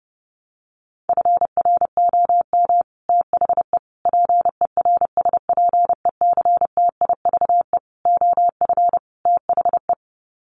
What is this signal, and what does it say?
Morse code, heard as a keyed tone, about 30 wpm, 700 Hz. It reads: FROM THE PERSPECTIVE OF THE
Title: Sets of 5 Words Encore2 30wpm